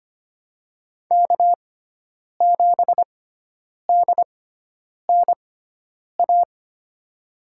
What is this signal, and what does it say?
Morse code, heard as a keyed tone, about 25 wpm, spaced as for 9 wpm, 700 Hz. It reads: K7DNA
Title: USA Call Signs - 5 Characters - 6x Character Spacing - 25wpm